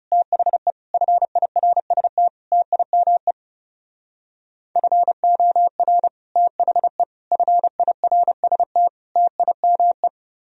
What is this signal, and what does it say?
Morse code, heard as a keyed tone, about 33 wpm, 700 Hz. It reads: THE FIRST TIME FOR THE FIRST TIME
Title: Speed-Racing - Sets of 4 Words 30wpm